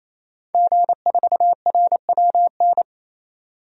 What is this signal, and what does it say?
Morse code, heard as a keyed tone, about 28 wpm, 700 Hz. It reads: G4RWN
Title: DXCC-100 Call Signs - 5 Characters - 28wpm